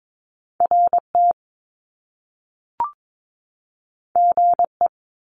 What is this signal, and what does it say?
Morse code, heard as a keyed tone, about 22 wpm, 700 Hz. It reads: RT GE
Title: Top Two Letter Combinations in Words 22wpm